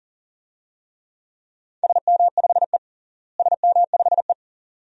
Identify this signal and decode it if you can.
Morse code, heard as a keyed tone, about 40 wpm, 700 Hz. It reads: SM5E SM5E
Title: DXCC-100 Call Signs - 4 Characters - Repeated 2x - 40wpm